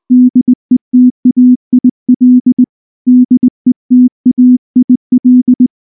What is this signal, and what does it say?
Morse code, heard as a keyed tone, about 19 wpm, 250 Hz. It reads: DETAIL DETAIL
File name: morse / morse-training-20WPM